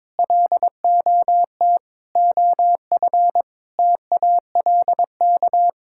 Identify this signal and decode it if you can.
Morse code, heard as a keyed tone, about 22 wpm, 700 Hz. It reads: LOT OF TALK
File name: practice / 22wpm / Sets of 8 Words - 22wpm